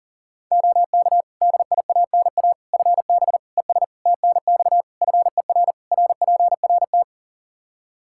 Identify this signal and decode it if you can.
Morse code, heard as a keyed tone, about 40 wpm, 700 Hz. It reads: OK DIANA FB ES TNX FER RPRT
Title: Practice QSO - Exchange 2 - Info - 40wpm